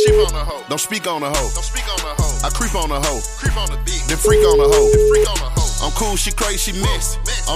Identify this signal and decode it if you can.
Ringback tone (Europe, 425 Hz), with noise.